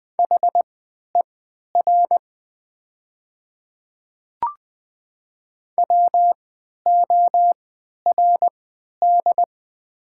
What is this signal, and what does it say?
Morse code, heard as a keyed tone, about 20 wpm, spaced as for 11 wpm, 700 Hz. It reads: HER WORD
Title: 3x Character Spacing - Top 100 Words - 20wpm